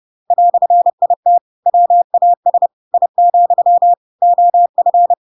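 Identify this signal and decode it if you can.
Morse code, heard as a keyed tone, about 30 wpm, 700 Hz. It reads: "IT WAS I, OF
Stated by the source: Chapter 2 of The Adventures of Pinocchio read as a book in Morse code (CW) at 30 wpm, by ik1hge.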